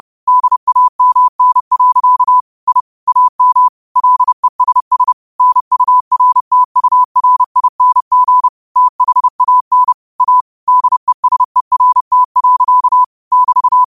Morse code, heard as a keyed tone, about 30 wpm, 1000 Hz. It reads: DAMN. I AM LESS NURTURING THAN A DESERT. <BT>
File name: qotdfiles / 30wpm